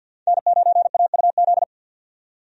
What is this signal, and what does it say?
Morse code, heard as a keyed tone, about 48 wpm, 700 Hz. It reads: N9AUZ
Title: USA Call Signs - 5 Characters - 50wpm